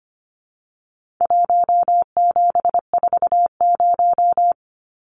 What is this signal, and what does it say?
Morse code, heard as a keyed tone, about 25 wpm, 700 Hz. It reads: 1740
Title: Numbers - 4-Digit 25wpm